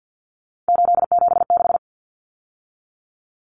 Morse code, heard as a keyed tone, about 50 wpm, 700 Hz. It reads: 876
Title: Numbers - 3-Digit 50wpm